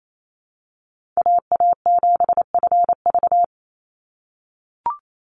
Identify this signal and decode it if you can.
Morse code, heard as a keyed tone, about 28 wpm, 700 Hz. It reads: AA7FV
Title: Call Signs - 28wpm